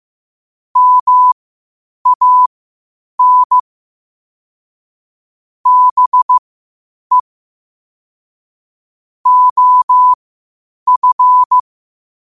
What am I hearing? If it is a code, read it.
Morse code, heard as a keyed tone, about 15 wpm, spaced as for 8 wpm, 1000 Hz. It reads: MAN BE OF
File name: qotdfiles / 8wpm